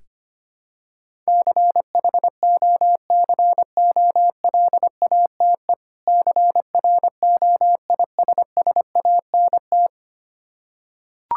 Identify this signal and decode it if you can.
Morse code, heard as a keyed tone, about 25 wpm, 700 Hz. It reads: CHOCOLATE CROISSANT
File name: practice / 25wpm / Difficult Words with Hints - 25wpm